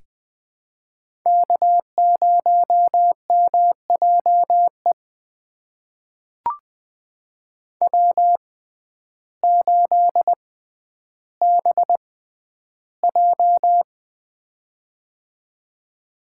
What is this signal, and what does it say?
Morse code, heard as a keyed tone, about 20 wpm, spaced as for 11 wpm, 700 Hz. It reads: K0MJE W8BJ
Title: USA Call Signs - 6x Character Spacing - 20wpm